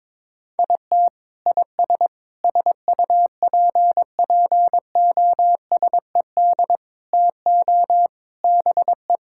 Morse code, heard as a keyed tone, about 22 wpm, 700 Hz. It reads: IT IS SUPPOSED TO BE
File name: practice / 22wpm / Sentences with 3-Word Phrases - 22wpm